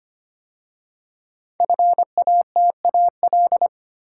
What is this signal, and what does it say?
Morse code, heard as a keyed tone, about 25 wpm, 700 Hz. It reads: FATAL
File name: 5 Letter Words - First 3 Letters Given - 25wpm